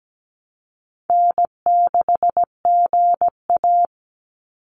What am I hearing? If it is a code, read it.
Morse code, heard as a keyed tone, about 17 wpm, 700 Hz. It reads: N6GA
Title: Easy Call Signs Repeated 2x - 17wpm